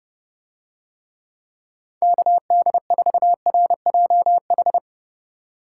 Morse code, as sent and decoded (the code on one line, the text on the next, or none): -.- -.. ....- .-. .--- ....
KD4RJH